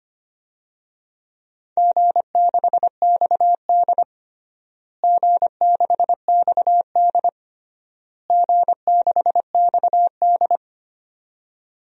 Morse code, heard as a keyed tone, about 25 wpm, 700 Hz. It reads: G6XD G6XD G6XD
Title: DXCC-100 Call Signs - 4 Characters - Repeated 3x - 25wpm